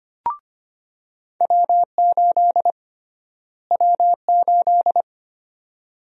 Morse code, heard as a keyed tone, about 25 wpm, 700 Hz. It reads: W8 W8
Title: USA Call Sign Prefixes - Repeated 2x - 25wpm